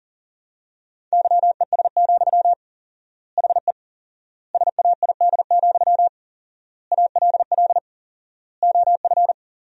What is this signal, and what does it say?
Morse code, heard as a keyed tone, about 37 wpm, 700 Hz. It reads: YES, HE SAID, ALL OF
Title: 4x Word-Spacing - Sentences from Top 500 Words - 40wpm